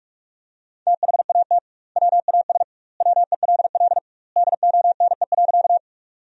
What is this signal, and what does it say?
Morse code, heard as a keyed tone, about 41 wpm, 700 Hz. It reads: THAT WAS WELL DONE.
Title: 2x Word-Spacing - Sentences from Top 500 Words - 45wpm